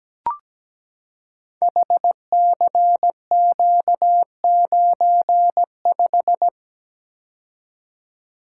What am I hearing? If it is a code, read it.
Morse code, heard as a keyed tone, about 17 wpm, 700 Hz. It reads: HCQ95
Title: Code Groups - 5 Letter:Numbers - 17wpm